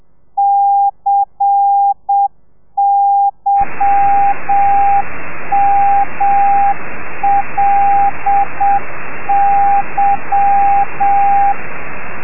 Morse code, heard as a keyed tone, about 7 wpm, 800 Hz, with radio noise. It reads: CYMLY
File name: telegrafia / i2tfj / Lez10 30